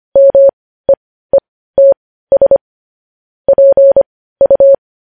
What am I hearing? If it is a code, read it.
Morse code, heard as a keyed tone, about 25 wpm, spaced as for 15 wpm, 550 Hz. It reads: MEETS PU
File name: podcast / News headlines-1x-25x15